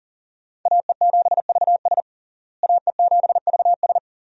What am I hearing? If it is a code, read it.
Morse code, heard as a keyed tone, about 40 wpm, 700 Hz. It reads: AE7VS AE7VS
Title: USA Call Signs - 5 Characters - Repeated 2x - 40wpm